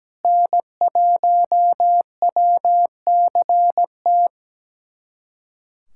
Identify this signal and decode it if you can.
Morse code, heard as a keyed tone, about 17 wpm, 700 Hz. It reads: N1WCT
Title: USA Callsign DE Callsign - 5 Characters - 17wpm